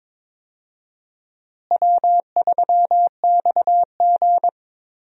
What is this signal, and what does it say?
Morse code, heard as a keyed tone, about 22 wpm, 700 Hz. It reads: W3XG
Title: USA Call Signs - 4 Characters - Repeated 2x - 22wpm